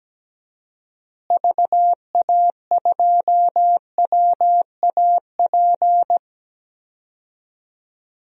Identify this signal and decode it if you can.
Morse code, heard as a keyed tone, about 17 wpm, 700 Hz. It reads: VA2WAP
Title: DXCC-100 Call Signs - 6 Characters - Repeated 2x - 17wpm